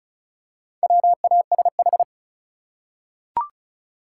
Morse code, heard as a keyed tone, about 36 wpm, 700 Hz. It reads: WASH
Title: Four Letter Words - 35wpm